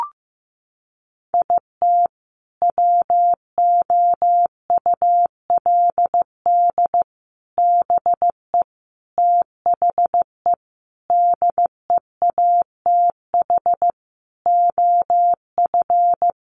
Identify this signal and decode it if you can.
Morse code, heard as a keyed tone, about 15 wpm, 700 Hz. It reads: IT WOULD BE THE DEATH OF
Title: Sentences from Top 1400 Words 15wpm